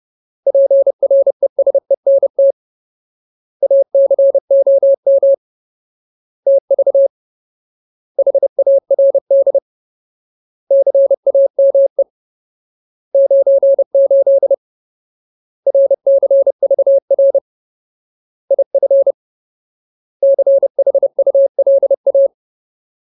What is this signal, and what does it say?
Morse code, heard as a keyed tone, about 30 wpm, 550 Hz. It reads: PRESENT ACOM TV HARD CAME 98 RCVR IF CHULA
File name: Ultimate Word List / 30WPM Ultimate